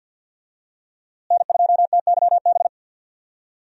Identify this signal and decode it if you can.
Morse code, heard as a keyed tone, about 52 wpm, 700 Hz. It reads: N2TYB
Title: USA Call Signs - 5 Characters - Repeated 3x - 50wpm